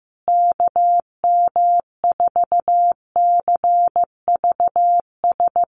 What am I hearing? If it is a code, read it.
Morse code, heard as a keyed tone, about 15 wpm, 700 Hz. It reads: KM4CVS
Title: USA Call Signs Repeated 3x 15wpm